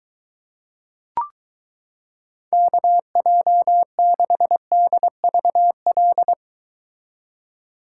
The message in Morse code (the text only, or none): KJ6DVL